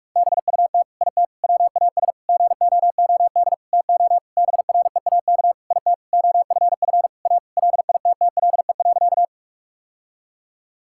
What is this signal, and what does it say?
Morse code, heard as a keyed tone, about 42 wpm, 700 Hz. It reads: BUT IT WAS GOOD TO BREAK IT OFF A LITTLE.
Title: Sentences from Top 1000 Words - Review Part 6 - 45wpm